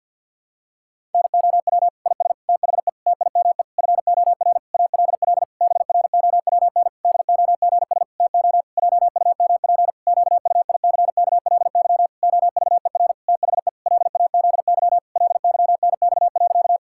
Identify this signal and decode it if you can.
Morse code, heard as a keyed tone, about 50 wpm, 700 Hz. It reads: NOW IS THE TIME FOR ALL BROWN DOGS TO JUMP QUICKLY OVER THE LAZY LYNX.
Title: Pangram Challenge v2 - 50wpm